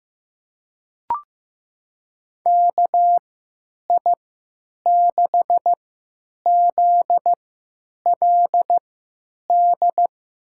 Morse code, heard as a keyed tone, about 15 wpm, spaced as for 9 wpm, 700 Hz. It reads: KI6ZLD
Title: USA Call Signs - 3x Character Spacing - 15wpm